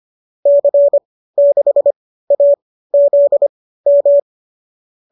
Morse code, heard as a keyed tone, about 25 wpm, spaced as for 15 wpm, 570 Hz. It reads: C6AZM
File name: qso204 15